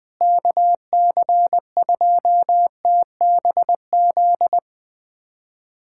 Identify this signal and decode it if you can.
Morse code, heard as a keyed tone, about 20 wpm, 700 Hz. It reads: KC2TBZ
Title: USA Callsign DE Callsign - 6 Characters - 20wpm